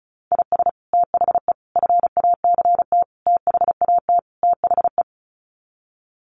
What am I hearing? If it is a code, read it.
Morse code, heard as a keyed tone, about 34 wpm, 700 Hz. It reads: IS THE FACT THAT THE
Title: Sets of 5 Words Encore2 35wpm